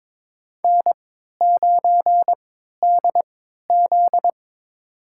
Morse code, heard as a keyed tone, about 22 wpm, spaced as for 13 wpm, 700 Hz. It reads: N9DZ
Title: USA Call Signs - 4 Characters - 3x Character Spacing - 22wpm